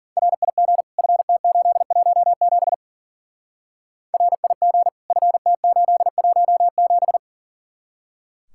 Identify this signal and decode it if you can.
Morse code, heard as a keyed tone, about 48 wpm, 700 Hz. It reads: RIG FT817 RIG FT817
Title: Speed-Racing Common QSO Phrases - 40wpm